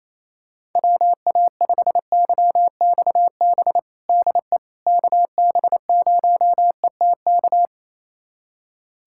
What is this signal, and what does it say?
Morse code, heard as a keyed tone, about 28 wpm, 700 Hz. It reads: WA5YXB DE KB0ETK
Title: USA Callsign DE Callsign - 6 Characters - 28wpm